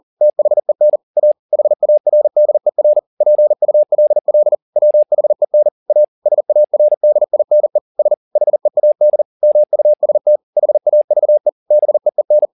Morse code, heard as a keyed tone, about 40 wpm, 600 Hz. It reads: THEN A HARDER PULL WHEN A SARDINE S HEAD MUST HAVE BEEN